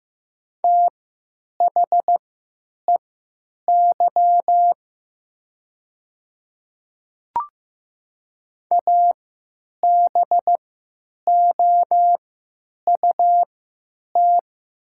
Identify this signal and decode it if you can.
Morse code, heard as a keyed tone, about 15 wpm, spaced as for 9 wpm, 690 Hz. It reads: THEY ABOUT
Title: 3x Character Spacing - Top 100 Words - 15wpm